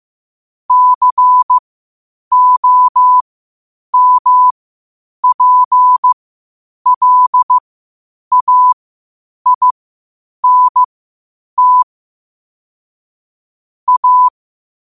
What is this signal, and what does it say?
Morse code, heard as a keyed tone, about 15 wpm, spaced as for 8 wpm, 1000 Hz. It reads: COMPLAINT A
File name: qotdfiles / 8wpm